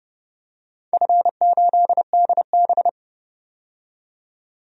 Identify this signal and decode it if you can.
Morse code, heard as a keyed tone, about 30 wpm, 700 Hz. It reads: F8DB
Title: DXCC-100 Call Signs - 4 Characters - 30wpm